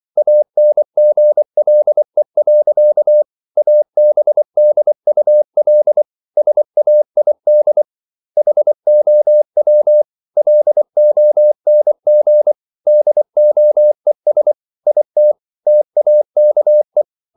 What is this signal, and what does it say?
Morse code, heard as a keyed tone, about 24 wpm, 600 Hz. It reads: ANGLE. ABDUL SAID HOW LONG DOES IT TAKE